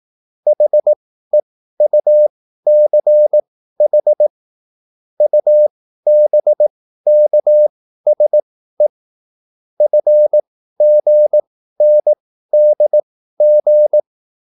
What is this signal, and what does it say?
Morse code, heard as a keyed tone, about 18 wpm, spaced as for 13 wpm, 600 Hz. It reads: HEUCH UBKSE FGNDG